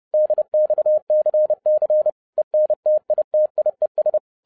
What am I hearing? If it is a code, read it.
Morse code, heard as a keyed tone, about 30 wpm, 610 Hz. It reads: DXCC ENTITIES
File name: 2012-01-mp3 / n11-I DIDNT MISS THE SOLAR CYCLE 24-30wpm